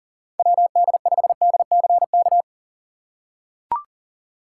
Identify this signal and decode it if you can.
Morse code, heard as a keyed tone, about 40 wpm, 700 Hz. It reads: WD5DCK
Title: USA Call Signs - 6 Characters - 4x Character Spacing - 40wpm